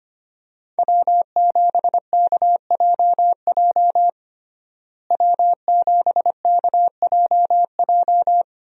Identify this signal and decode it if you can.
Morse code, heard as a keyed tone, about 25 wpm, 700 Hz. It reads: W7KJJ W7KJJ
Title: USA Call Signs - 5 Characters - Repeated 2x - 25wpm